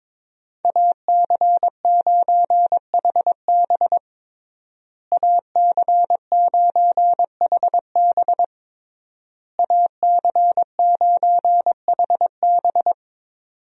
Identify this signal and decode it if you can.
Morse code, heard as a keyed tone, about 22 wpm, 700 Hz. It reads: AC9HB AC9HB AC9HB
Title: USA Call Signs - 5 Characters - Repeated 3x - 22wpm